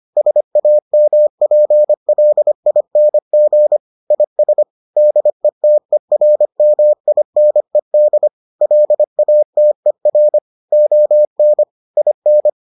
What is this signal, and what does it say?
Morse code, heard as a keyed tone, about 25 wpm, 600 Hz. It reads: SAMPLING IS DETERMINED LATER ON IN